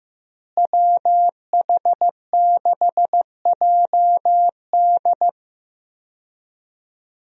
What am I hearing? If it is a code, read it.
Morse code, heard as a keyed tone, about 15 wpm, 700 Hz. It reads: WH6JD
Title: USA Call Signs - 5 Characters - Repeated 3x - 15wpm